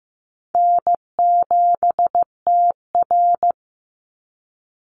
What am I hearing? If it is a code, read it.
Morse code, heard as a keyed tone, about 15 wpm, 700 Hz. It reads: N7TR
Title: Call Signs Repeated 2x - 15wpm